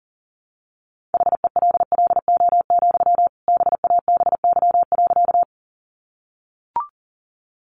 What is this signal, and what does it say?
Morse code, heard as a keyed tone, about 40 wpm, 700 Hz. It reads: HELLO, BABY.
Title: Sentences from Top 1300 Words 40wpm